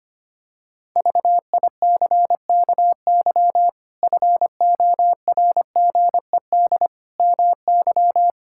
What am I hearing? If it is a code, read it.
Morse code, heard as a keyed tone, about 25 wpm, 700 Hz. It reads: VICKY FORGED MY
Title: Pangram Challenge v2 - 25wpm